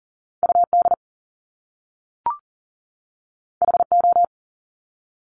Morse code, heard as a keyed tone, about 41 wpm, 700 Hz. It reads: UD HO
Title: Top Two Letter Combinations in Words 40wpm